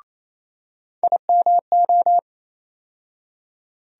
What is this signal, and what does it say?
Morse code, heard as a keyed tone, about 28 wpm, 700 Hz. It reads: IMO
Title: USA Call Sign Postfixes - 3 Letters 28wpm